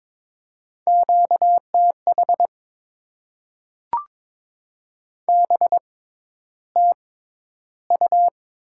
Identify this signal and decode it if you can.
Morse code, heard as a keyed tone, about 22 wpm, spaced as for 11 wpm, 700 Hz. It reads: QTH BTU
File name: Top 20 QSO Elements - 6x Character Spacing - 22wpm